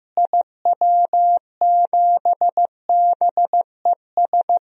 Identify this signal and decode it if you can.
Morse code, heard as a keyed tone, about 15 wpm, 700 Hz. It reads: IW7BES
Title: DXCC-100 Call Signs - 6 Characters - Repeated 3x - 15wpm